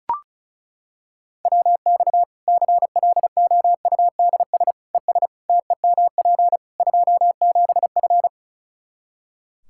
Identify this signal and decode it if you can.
Morse code, heard as a keyed tone, about 36 wpm, 700 Hz. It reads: WX CLOUDS ES TEMP 27F
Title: Practice QSO - Exchange 2 - Info - 35wpm